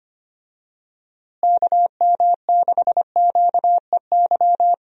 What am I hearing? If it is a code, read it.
Morse code, heard as a keyed tone, about 25 wpm, 700 Hz. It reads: KM6QEY
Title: USA Call Signs - 6 Characters - Repeated 2x - 25wpm